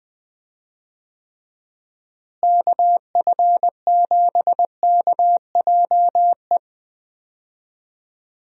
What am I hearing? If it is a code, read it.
Morse code, heard as a keyed tone, about 20 wpm, 700 Hz. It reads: KF7KJE
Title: USA Call Signs - 6 Characters - Repeated 2x - 20wpm